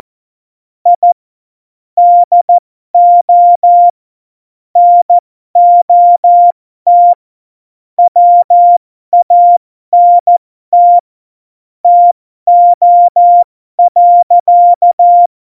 Morse code, heard as a keyed tone, about 14 wpm, spaced as for 12 wpm, 700 Hz. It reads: I DO NOT WANT TO.
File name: Morsekurs / Witze